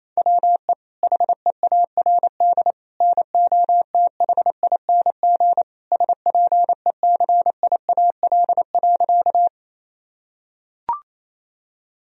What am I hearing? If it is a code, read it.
Morse code, heard as a keyed tone, about 28 wpm, 700 Hz. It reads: WE HEARD NOTHING SPECIAL.
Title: Sentences from Top 1000 Words - Review Part 3 - 28wpm